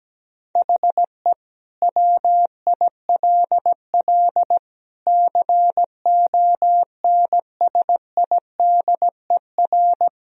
Morse code, heard as a keyed tone, about 17 wpm, 700 Hz. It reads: HE WILL CONSIDER
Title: Sentences from Top 1000 Words - Review Part 4 - 17wpm